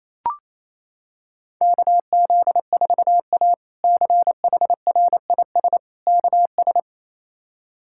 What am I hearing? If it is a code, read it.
Morse code, heard as a keyed tone, about 27 wpm, 700 Hz. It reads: KZ4A CHRIS KS
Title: K1USN SST QSO Practice - 28wpm